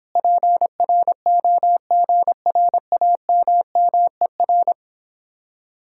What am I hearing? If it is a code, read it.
Morse code, heard as a keyed tone, about 29 wpm, 700 Hz. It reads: PROGRAMMER